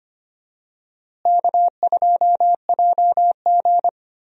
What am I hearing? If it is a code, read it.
Morse code, heard as a keyed tone, about 25 wpm, 700 Hz. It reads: K2JG
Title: USA Call Signs - 4 Characters - Repeated 2x - 25wpm